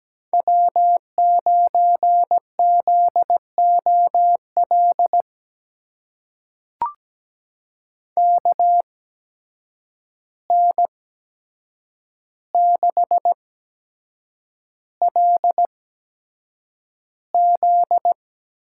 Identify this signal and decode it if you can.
Morse code, heard as a keyed tone, about 17 wpm, spaced as for 7 wpm, 700 Hz. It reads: W9ZOL KN6LZ
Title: USA Call Signs - 5 Characters - 8x Character Spacing - 17wpm